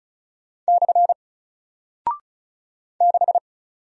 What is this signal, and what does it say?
Morse code, heard as a keyed tone, about 36 wpm, 700 Hz. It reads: / 6
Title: Single Character (Letter:Number:Punctuation:Prosign) Alt 35wpm